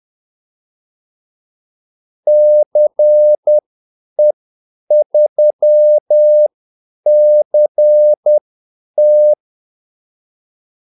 Morse code, heard as a keyed tone, about 10 wpm, spaced as for 8 wpm, 600 Hz. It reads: CE3CT
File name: Callsigns-10-8wpm